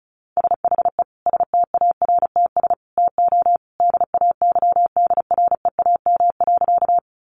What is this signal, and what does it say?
Morse code, heard as a keyed tone, about 34 wpm, 700 Hz. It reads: SHE STARTS TO DAYDREAM.
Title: Easy Phrases 35wpm